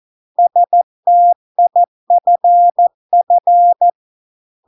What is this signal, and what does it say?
Morse code, heard as a keyed tone, about 14 wpm, 700 Hz. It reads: STIFF